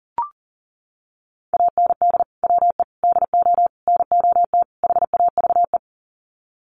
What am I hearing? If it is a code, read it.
Morse code, heard as a keyed tone, about 40 wpm, 700 Hz. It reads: AND WE DO NOT HAVE
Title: Sets of 5 Words Encore 40wpm